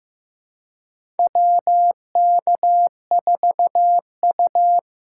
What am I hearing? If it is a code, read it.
Morse code, heard as a keyed tone, about 15 wpm, 700 Hz. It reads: WK4U
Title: USA Call Signs - 3x Character Spacing - 15wpm